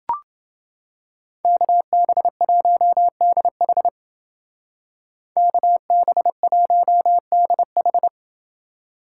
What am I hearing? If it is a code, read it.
Morse code, heard as a keyed tone, about 28 wpm, 700 Hz. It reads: KB1DH KB1DH